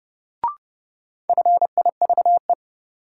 Morse code, heard as a keyed tone, about 30 wpm, 700 Hz. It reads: FIVE
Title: Sentences from Top 400 Words - New Word Drill - 30wpm